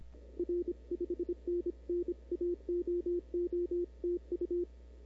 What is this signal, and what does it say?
Morse code, heard as a keyed tone, about 29 wpm, 340 Hz. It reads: R 5NN AOO TU
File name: CW / 2005-k5tr arrl cw dx -00:00